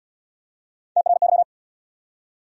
Morse code, heard as a keyed tone, about 69 wpm, 700 Hz. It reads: TRY